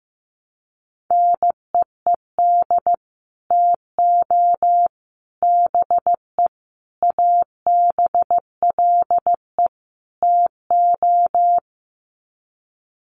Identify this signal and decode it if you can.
Morse code, heard as a keyed tone, about 15 wpm, 700 Hz. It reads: NEED TO BE ABLE TO
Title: Sets of 5 Words Encore 15wpm